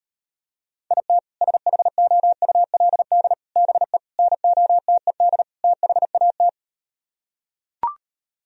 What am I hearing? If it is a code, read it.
Morse code, heard as a keyed tone, about 38 wpm, 700 Hz. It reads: IT SHOULD BE NOTED THAT